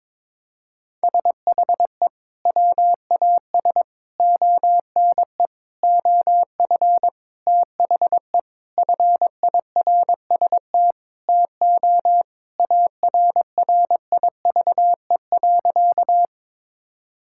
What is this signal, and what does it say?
Morse code, heard as a keyed tone, about 22 wpm, 700 Hz. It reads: SHE WAS ONE OF THE FIRST TO ARRIVE.
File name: practice / 22wpm / Sentences with 4-Word Phrases - 22wpm